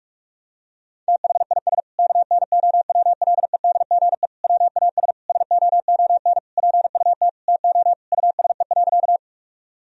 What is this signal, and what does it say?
Morse code, heard as a keyed tone, about 46 wpm, 700 Hz. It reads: THIS KNOWLEDGE WAS SOON PUT TO USE.
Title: Sentences from Top 1000 Words - Review Part 3 - 45wpm